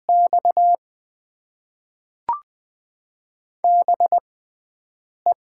Morse code, heard as a keyed tone, about 20 wpm, spaced as for 7 wpm, 700 Hz. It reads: X BE
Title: Top 90 QSO Elements - 6x Character Spacing - 20wpm